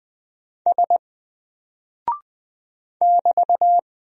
Morse code, heard as a keyed tone, about 20 wpm, 700 Hz. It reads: S <BT>
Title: Single Character (Letter:Number:Punctuation:Prosign) 20wpm